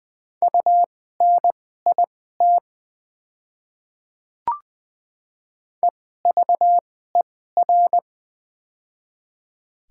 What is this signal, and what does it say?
Morse code, heard as a keyed tone, about 20 wpm, spaced as for 15 wpm, 700 Hz. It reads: UNIT EVER
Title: 2x Character Spacing - Top 500 Words - 20wpm